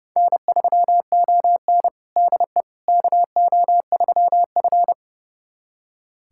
Morse code, heard as a keyed tone, about 30 wpm, 700 Hz. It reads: N3ON DE KO3F
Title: USA Callsign DE Callsign - 4 Characters - 30wpm